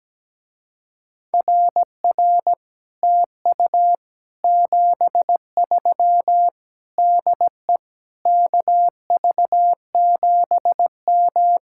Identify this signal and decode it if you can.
Morse code, heard as a keyed tone, about 17 wpm, 700 Hz. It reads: RR TU 73 DE KV7M
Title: Speed-Racing Common QSO Phrases - 17wpm